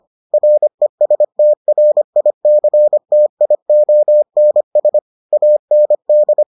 Morse code, heard as a keyed tone, about 25 wpm, 600 Hz. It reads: RESTRICTIONS AND